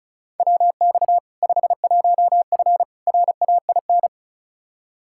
Morse code, heard as a keyed tone, about 36 wpm, 700 Hz. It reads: WX 51F RAIN
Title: K1USN SST QSO Practice 35wpm